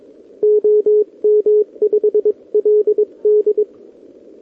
Morse code, heard as a keyed tone, about 22 wpm, 440 Hz. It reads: OM5LD
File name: OK QRP 2026 / OKQRP2026